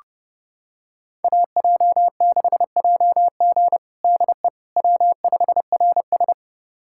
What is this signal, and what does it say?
Morse code, heard as a keyed tone, about 30 wpm, 700 Hz. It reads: AJ6JG DE W5RS
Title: USA Callsign DE Callsign 30wpm